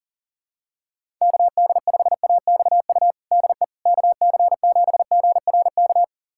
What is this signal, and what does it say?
Morse code, heard as a keyed tone, about 40 wpm, 700 Hz. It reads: KD5AXU DE KC7GRK
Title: USA Callsign DE Callsign - 6 Characters - 40wpm